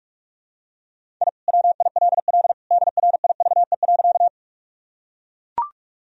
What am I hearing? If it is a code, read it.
Morse code, heard as a keyed tone, about 41 wpm, 700 Hz. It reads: I WILL DRIVE.
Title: Sentences from Top 1000 Words - Review Part 2 - 45wpm